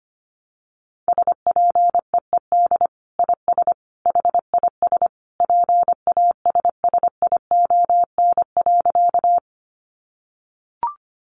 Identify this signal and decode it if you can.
Morse code, heard as a keyed tone, about 25 wpm, 700 Hz. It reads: SPEED IS HIS PASSION.
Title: Sentences from Top 1300 Words 25wpm